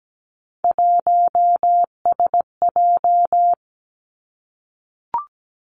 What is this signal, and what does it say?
Morse code, heard as a keyed tone, about 17 wpm, 700 Hz. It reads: 1SJ
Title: Three Challenging Letter-Number Combinations 17wpm